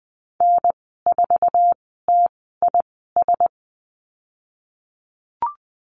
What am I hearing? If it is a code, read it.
Morse code, heard as a keyed tone, about 20 wpm, spaced as for 15 wpm, 700 Hz. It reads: N4TIS
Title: 048 Lesson 9g - Number 4 - Call Signs 20wpm